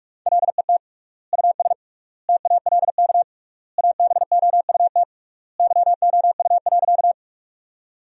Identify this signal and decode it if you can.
Morse code, heard as a keyed tone, about 43 wpm, 700 Hz. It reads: LET US TALK ABOUT YOU.
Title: 3x Word-Spacing - Sentences from Top 400 Words - 45wpm